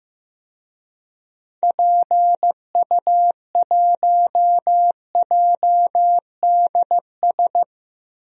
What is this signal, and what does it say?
Morse code, heard as a keyed tone, about 15 wpm, 700 Hz. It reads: PU1JDS
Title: DXCC-100 Call Signs - 6 Characters - Repeated 2x - 15wpm